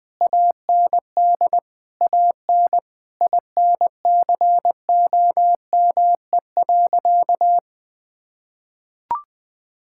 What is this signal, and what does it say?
Morse code, heard as a keyed tone, about 20 wpm, 700 Hz. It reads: AND AN INCOME.
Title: Sentences from Top 1000 Words - Review Part 5 - 20wpm